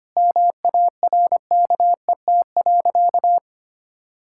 Morse code, heard as a keyed tone, about 25 wpm, 700 Hz. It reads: MARKET.
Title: Sentences from Top 1600 Words 25wpm